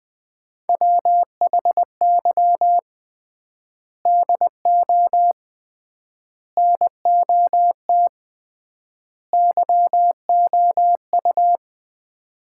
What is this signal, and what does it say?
Morse code, heard as a keyed tone, about 21 wpm, 700 Hz. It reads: WHY DO NOT YOU
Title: 3x Word-Spacing - Sets of 4 Words - 20wpm